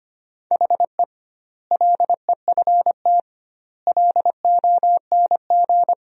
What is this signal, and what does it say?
Morse code, heard as a keyed tone, about 25 wpm, 700 Hz. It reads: HE LEFT LONG
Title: 2x Word-Spacing - Sentences from Top 500 Words - 25wpm